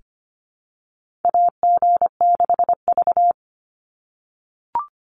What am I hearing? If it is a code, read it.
Morse code, heard as a keyed tone, about 25 wpm, 700 Hz. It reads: AG6V
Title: NA SOTA Chaser Callsigns 25wpm